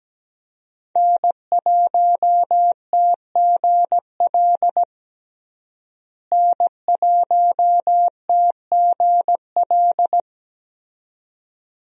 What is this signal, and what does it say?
Morse code, heard as a keyed tone, about 17 wpm, 700 Hz. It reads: N1TGL N1TGL
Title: USA Call Signs - 5 Characters - Repeated 2x - 17wpm